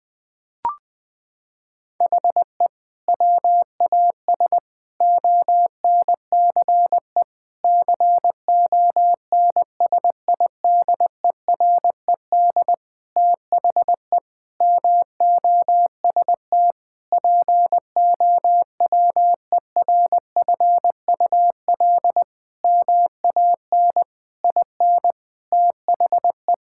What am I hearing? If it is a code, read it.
Morse code, heard as a keyed tone, about 20 wpm, 700 Hz. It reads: HE WAS ONCE CONSIDERED THE MOST POWERFUL MAN IN THE
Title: Sentences with 7-Word Phrases - 20wpm